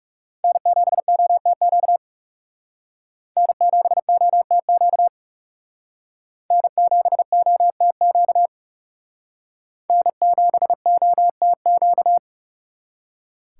Morse code, heard as a keyed tone, about 38 wpm, 700 Hz. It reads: N7OTQ N7OTQ N7OTQ N7OTQ
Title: Speed Racing - USA Call Signs 30wpm